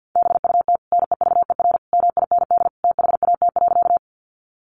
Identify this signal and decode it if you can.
Morse code, heard as a keyed tone, about 50 wpm, 700 Hz. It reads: BUT NEVER MIND THAT.
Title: Sentences from Top 500 Words 50wpm